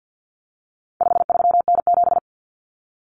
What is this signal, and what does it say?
Morse code, heard as a keyed tone, about 50 wpm, 700 Hz. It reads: 53N7
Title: Common Cut Numbers - 4-Digit 50wpm